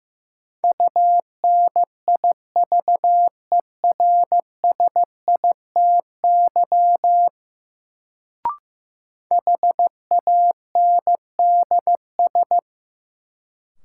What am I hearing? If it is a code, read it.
Morse code, heard as a keyed tone, about 15 wpm, 700 Hz. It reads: UNIVERSITY HANDS
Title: Sets of 4 Words - New Word Drill - 15wpm